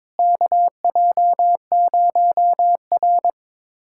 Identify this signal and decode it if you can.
Morse code, heard as a keyed tone, about 21 wpm, 700 Hz. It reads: KJ0R
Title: USA Callsign DE Callsign - 4 Characters - 22wpm